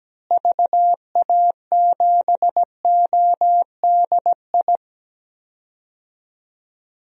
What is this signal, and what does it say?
Morse code, heard as a keyed tone, about 17 wpm, 700 Hz. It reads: VA7ODI
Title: DXCC-100 Call Signs - 6 Characters - Repeated 2x - 17wpm